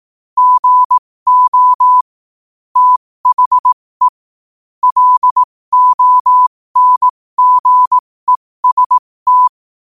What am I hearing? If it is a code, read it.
Morse code, heard as a keyed tone, about 18 wpm, spaced as for 15 wpm, 1000 Hz. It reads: GO THE LONGEST
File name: qotdfiles / 15wpm